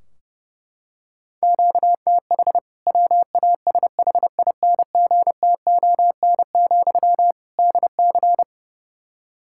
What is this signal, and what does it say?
Morse code, heard as a keyed tone, about 32 wpm, 700 Hz. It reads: QTH WASHINGTON, DC
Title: K1USN SST QSO Practice 30wpm